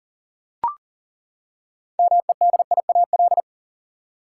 Morse code, heard as a keyed tone, about 40 wpm, 700 Hz. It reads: MEDIAL
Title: Practice QSO - Exchange 3 - More Info - 40wpm